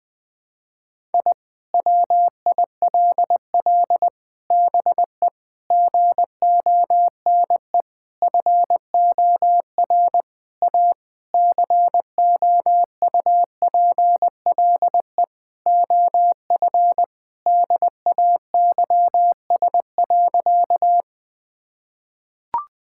Morse code, heard as a keyed tone, about 20 wpm, 700 Hz. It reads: I WILL BE GONE FOR A COUPLE OF DAYS.
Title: Sentences with 4-Word Phrases - 20wpm